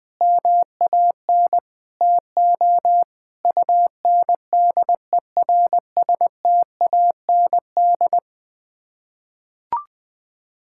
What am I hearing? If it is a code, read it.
Morse code, heard as a keyed tone, about 20 wpm, 700 Hz. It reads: MAN TO UNDERSTAND
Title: Sets of 9 Words - 20wpm